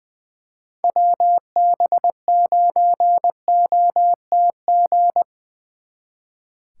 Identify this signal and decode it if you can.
Morse code, heard as a keyed tone, about 23 wpm, 700 Hz. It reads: WB9OTG
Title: Speed Racing - USA 2x3 Call Signs 20wpm